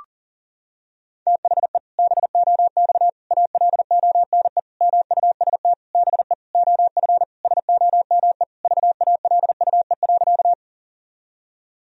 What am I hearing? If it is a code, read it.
Morse code, heard as a keyed tone, about 37 wpm, 700 Hz. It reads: THE BOX ALONE MUST BE OF SOME VALUE.
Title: Sentences from Top 1000 Words - Review Part 3 - 40wpm